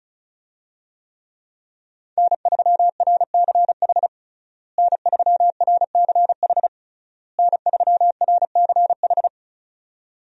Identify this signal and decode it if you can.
Morse code, heard as a keyed tone, about 34 wpm, 700 Hz. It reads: N3RCH N3RCH N3RCH
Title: USA Call Signs - 5 Characters - Repeated 3x - 35wpm